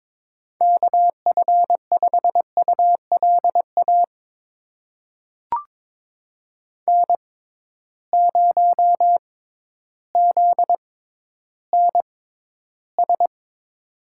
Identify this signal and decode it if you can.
Morse code, heard as a keyed tone, about 22 wpm, spaced as for 12 wpm, 700 Hz. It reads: KF5ULA N0ZNS
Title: USA Call Signs - 6x Character Spacing - 22wpm